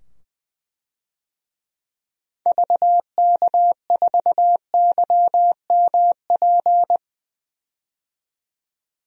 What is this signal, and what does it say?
Morse code, heard as a keyed tone, about 20 wpm, 700 Hz. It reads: VK4YMP
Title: DXCC-100 Call Signs - 6 Characters - 20wpm